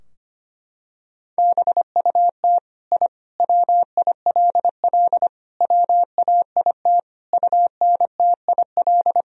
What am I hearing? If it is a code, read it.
Morse code, heard as a keyed tone, about 25 wpm, 700 Hz. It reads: BUT I WILL WAIT UNTIL
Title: Sentences from Top 1000 Words - Review Part 3 - 25wpm